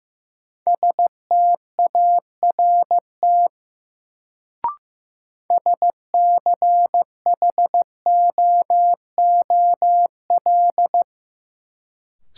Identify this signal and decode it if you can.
Morse code, heard as a keyed tone, about 15 wpm, 700 Hz. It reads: START SCHOOL
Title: Sentences from Top 300 Words - New Word Drill - 15wpm